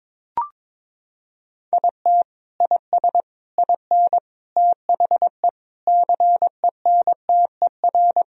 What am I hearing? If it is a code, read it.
Morse code, heard as a keyed tone, about 22 wpm, 700 Hz. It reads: IT IS IN THE CENTER
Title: Sentences from Top 1000 Words - Review Part 2 - 22wpm